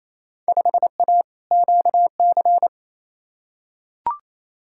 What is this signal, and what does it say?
Morse code, heard as a keyed tone, about 28 wpm, 700 Hz. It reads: 5A QC
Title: Field Day Exchanges Encore - 28wpm